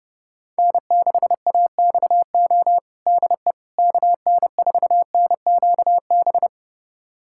Morse code, heard as a keyed tone, about 30 wpm, 700 Hz. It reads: N6AXO DE KN4NQB
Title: USA Callsign DE Callsign 30wpm